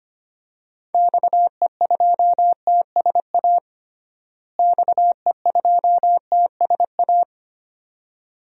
Morse code, heard as a keyed tone, about 25 wpm, 700 Hz. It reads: XE2TSA XE2TSA
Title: DXCC-100 Call Signs - 6 Characters - Repeated 2x - 25wpm